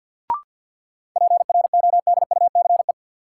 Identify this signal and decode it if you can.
Morse code, heard as a keyed tone, about 53 wpm, 700 Hz. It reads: PRODUCE
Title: Sentences from Top 1000 Words - Review Part 3 - New Word Drill - 50wpm